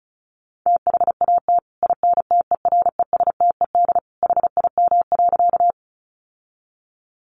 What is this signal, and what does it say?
Morse code, heard as a keyed tone, about 34 wpm, 700 Hz. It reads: THAT INTERESTED HIM.
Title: Sentences from Top 900 Words 35wpm